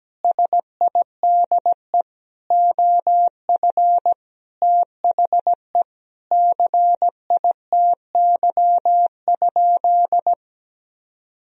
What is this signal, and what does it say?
Morse code, heard as a keyed tone, about 17 wpm, 700 Hz. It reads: SIDE OF THE CITY?
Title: Sentences from Top 1000 Words - Review Part 3 - 17wpm